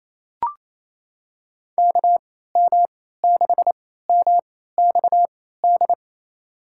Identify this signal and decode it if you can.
Morse code, heard as a keyed tone, about 28 wpm, spaced as for 16 wpm, 700 Hz. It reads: KM6MXD
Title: USA Call Signs - 6 Characters - 3x Character Spacing - 28wpm